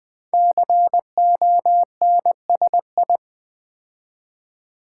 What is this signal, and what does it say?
Morse code, heard as a keyed tone, about 20 wpm, 700 Hz. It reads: CONSI
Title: Build A Word - 5 Letters At A Time - 12-Letter Words - 20wpm